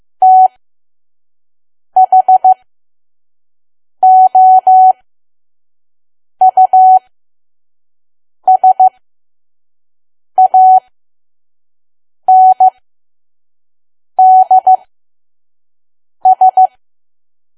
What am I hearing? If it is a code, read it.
Morse code, heard as a keyed tone, about 15 wpm, spaced as for 5 wpm, 750 Hz. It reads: THOUSANDS
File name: Archive / 5 WPM files / PM